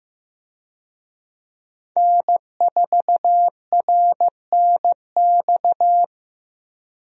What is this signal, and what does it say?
Morse code, heard as a keyed tone, about 15 wpm, 700 Hz. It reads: N4RNX
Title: USA Call Signs - 5 Characters - Repeated 2x - 15wpm